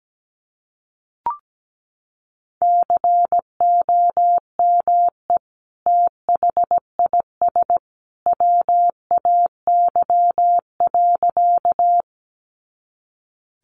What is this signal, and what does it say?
Morse code, heard as a keyed tone, about 17 wpm, 700 Hz. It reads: COME THIS WAY.
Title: Sentences from Top 100 Words 17wpm